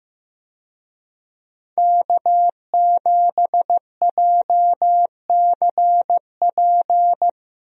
Morse code, heard as a keyed tone, about 15 wpm, 700 Hz. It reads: K7JCP
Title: USA Call Signs - 5 Characters - Repeated 3x - 15wpm